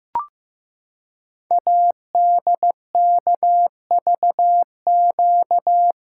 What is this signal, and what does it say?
Morse code, heard as a keyed tone, about 15 wpm, 700 Hz. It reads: ADKVQ
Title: Code Groups - 5 Letters - 15wpm